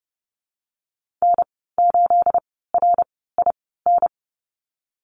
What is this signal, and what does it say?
Morse code, heard as a keyed tone, about 30 wpm, spaced as for 17 wpm, 700 Hz. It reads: N8RIN
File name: Lesson 37g - Letter x - Call Signs 30wpm